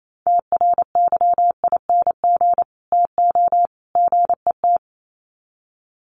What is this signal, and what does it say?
Morse code, heard as a keyed tone, about 28 wpm, 700 Hz. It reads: TRYING TO GET
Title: Sets of 3 Words 28wpm